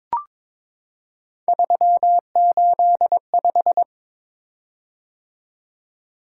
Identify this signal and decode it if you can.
Morse code, heard as a keyed tone, about 22 wpm, 700 Hz. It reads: 385
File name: practice / 22wpm / Three Letter-Number Combinations - 22wpm